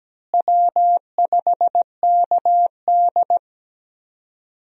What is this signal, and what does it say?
Morse code, heard as a keyed tone, about 17 wpm, 700 Hz. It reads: W5KD
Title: USA Call Signs - 4 Characters - Repeated 3x - 17wpm